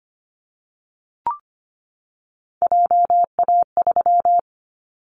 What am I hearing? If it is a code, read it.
Morse code, heard as a keyed tone, about 25 wpm, 700 Hz. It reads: JA3
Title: Callsign Prefixes for Top 50 DXCC - 25wpm